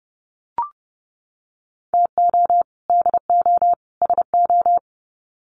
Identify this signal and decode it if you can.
Morse code, heard as a keyed tone, about 30 wpm, 700 Hz. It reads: TO DO SO
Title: Sets of 3 Words 30wpm